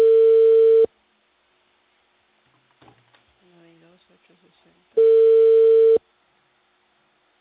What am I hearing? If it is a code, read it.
Ringback tone (China, 450 Hz), with speech.